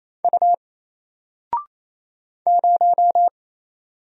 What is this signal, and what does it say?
Morse code, heard as a keyed tone, about 27 wpm, 700 Hz. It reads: U 0
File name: Single-Letter-Number-28wpm